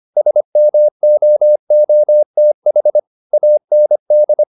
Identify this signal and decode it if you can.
Morse code, heard as a keyed tone, about 25 wpm, 600 Hz. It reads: SMOOTH AND